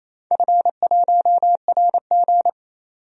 Morse code, heard as a keyed tone, about 29 wpm, 700 Hz. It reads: F1RG
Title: DXCC-100 Call Signs - 4 Characters - 28wpm